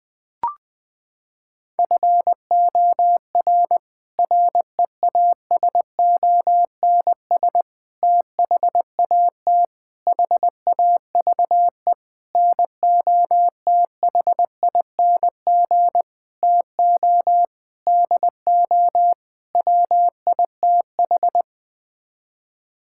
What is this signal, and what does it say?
Morse code, heard as a keyed tone, about 21 wpm, 700 Hz. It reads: FOR REASONS THAT HAVE NOTHING TO DO WITH
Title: Sets of 8 Words - 20wpm